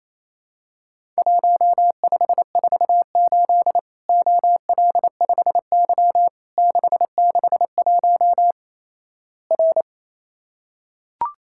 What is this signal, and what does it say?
Morse code, heard as a keyed tone, about 28 wpm, 700 Hz. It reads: 1548 OL5Y 661 R
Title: Worked All Europe DX Contest - QTC Exchange - 28wpm